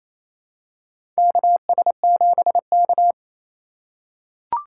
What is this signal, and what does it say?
Morse code, heard as a keyed tone, about 27 wpm, 700 Hz. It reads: KS7K
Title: USA Call Signs - 4 Characters - 4x Character Spacing - 28wpm